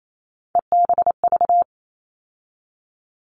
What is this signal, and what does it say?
Morse code, heard as a keyed tone, about 28 wpm, 700 Hz. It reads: EBV
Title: All Cut Numbers - 3-Digit 28wpm